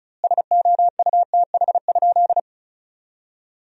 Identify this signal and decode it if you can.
Morse code, heard as a keyed tone, about 36 wpm, 700 Hz. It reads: SOUTH?
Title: 2x Word-Spacing - Sentences from Top 400 Words - 35wpm